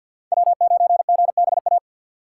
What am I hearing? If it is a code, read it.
Morse code, heard as a keyed tone, about 65 wpm, 700 Hz. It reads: W9GBA